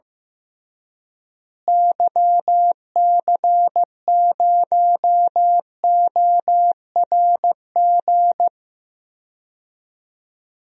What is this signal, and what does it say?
Morse code, heard as a keyed tone, about 15 wpm, 700 Hz. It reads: YC0ORG
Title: DXCC-100 Call Signs - 6 Characters - Repeated 2x - 15wpm